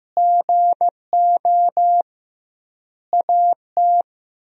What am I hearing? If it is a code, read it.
Morse code, heard as a keyed tone, about 15 wpm, 700 Hz. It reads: GO AT
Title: 2x Word-Spacing - Sentences from Top 300 Words - 15wpm